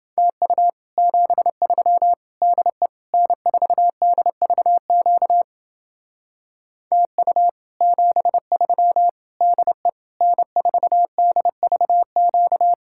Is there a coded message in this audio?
Yes: Morse code from 0.2 to 12.7 s, heard as a keyed tone, about 29 wpm, 700 Hz. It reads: TU 73 DE N4DVQ TU 73 DE N4DVQ